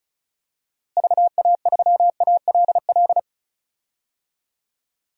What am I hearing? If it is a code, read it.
Morse code, heard as a keyed tone, about 36 wpm, 700 Hz. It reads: VA3ALL
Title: DXCC-100 Call Signs - 6 Characters - Repeated 2x - 35wpm